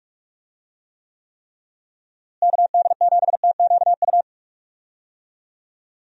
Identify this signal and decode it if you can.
Morse code, heard as a keyed tone, about 41 wpm, 700 Hz. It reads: KD7TQU
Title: USA Call Signs - 6 Characters - Repeated 3x - 45wpm